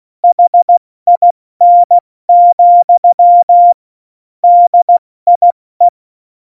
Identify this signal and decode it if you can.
Morse code, heard as a keyed tone, about 16 wpm, 700 Hz. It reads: HIN, DIE
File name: Morsekurs / Witze / Witz 22. Oktober 2019, 70 F80